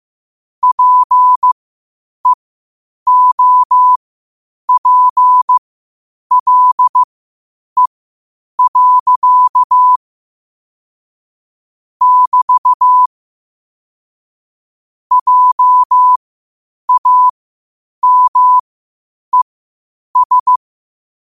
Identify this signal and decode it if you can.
Morse code, heard as a keyed tone, about 15 wpm, spaced as for 8 wpm, 1000 Hz. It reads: PEOPLE. <BT> JAMES
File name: qotdfiles / 8wpm